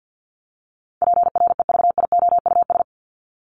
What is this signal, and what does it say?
Morse code, heard as a keyed tone, about 50 wpm, 700 Hz. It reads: PREVIOUS
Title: LY Suffix - Top 100 Words with Base Words 50wpm